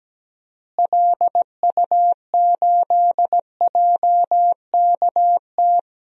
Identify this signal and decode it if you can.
Morse code, heard as a keyed tone, about 17 wpm, 700 Hz. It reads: LU8JKT
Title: DXCC-100 Call Signs - 6 Characters - Repeated 3x - 17wpm